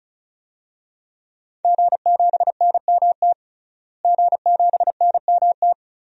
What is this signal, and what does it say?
Morse code, heard as a keyed tone, about 36 wpm, 700 Hz. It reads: G7NMT G7NMT
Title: DXCC-100 Call Signs - 5 Characters - Repeated 3x - 35wpm